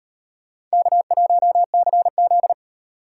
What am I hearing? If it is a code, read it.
Morse code, heard as a keyed tone, about 38 wpm, 700 Hz. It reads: K1CZ